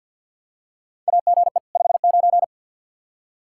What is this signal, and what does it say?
Morse code, heard as a keyed tone, about 51 wpm, 700 Hz. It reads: AGE 59
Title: K1USN SST QSO Practice 50wpm